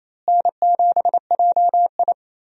Morse code, heard as a keyed tone, about 28 wpm, 700 Hz. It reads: N7JI
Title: USA Call Signs - 4 Characters - Repeated 3x - 28wpm